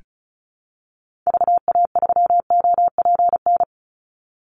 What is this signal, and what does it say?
Morse code, heard as a keyed tone, about 34 wpm, 700 Hz. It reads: VA3OPN
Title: Call Signs Repeated 3x - 35wpm